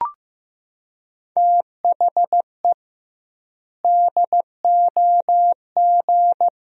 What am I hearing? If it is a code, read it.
Morse code, heard as a keyed tone, about 15 wpm, 700 Hz. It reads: THE DOG
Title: 2x Word-Spacing - Sentences from Top 400 Words - 15wpm